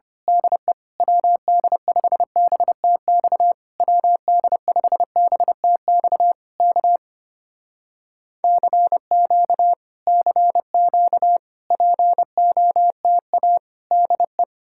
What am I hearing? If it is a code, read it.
Morse code, heard as a keyed tone, about 24 wpm, 700 Hz. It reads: DE WD5BTX WD5BTX K CQ CQ POTA DE
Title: Speed-Racing Common QSO Phrases - 25wpm